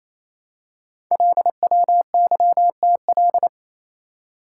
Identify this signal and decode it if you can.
Morse code, heard as a keyed tone, about 29 wpm, 700 Hz. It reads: LWYTL
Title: Code Groups - 5 Letters - 28wpm